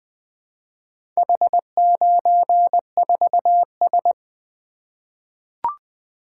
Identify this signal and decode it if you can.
Morse code, heard as a keyed tone, about 20 wpm, 700 Hz. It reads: H94S
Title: Four Letter-Number Combinations - 20wpm